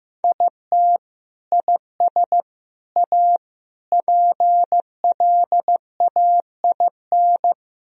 Morse code, heard as a keyed tone, about 16 wpm, 700 Hz. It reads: IT IS A PLAIN
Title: Sentences from Top 1000 Words - Review Part 2 - 15wpm